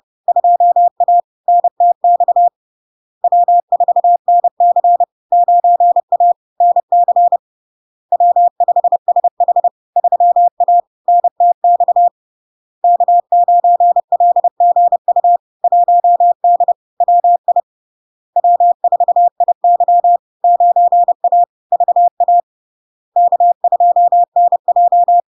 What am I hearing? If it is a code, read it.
Morse code, heard as a keyed tone, about 30 wpm, 700 Hz. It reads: 2A NTX W4NC 9A NC W5SH 3A NTX K9LGU 1D WI W4IY 9A VA K2NJ
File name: LARK-FD-2022-Log-in-CW-30wpm-700-Hz